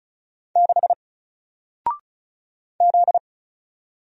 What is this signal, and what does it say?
Morse code, heard as a keyed tone, about 36 wpm, 700 Hz. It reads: 6 Z
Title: Single Character (Letter:Number:Punctuation:Prosign) Alt 35wpm